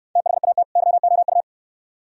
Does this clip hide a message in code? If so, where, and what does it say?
Yes: Morse code from 0.2 to 1.4 s, heard as a keyed tone, about 64 wpm, 690 Hz. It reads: THAT YOU